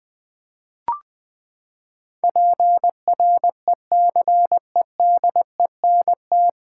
Morse code, heard as a keyed tone, about 20 wpm, 700 Hz. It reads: PRECEDENT
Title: Nine Letter Words - 20wpm